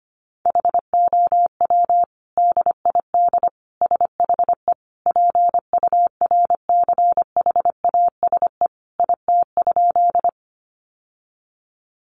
Morse code, heard as a keyed tone, about 25 wpm, 700 Hz. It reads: HOW DID SHE PURCHASE IT?
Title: Sentences from Top 800 Words 25wpm